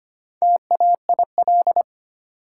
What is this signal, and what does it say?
Morse code, heard as a keyed tone, about 25 wpm, 700 Hz. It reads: TAIL
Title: Sentences from Top 1000 Words - Review Part 3 - New Word Drill - 25wpm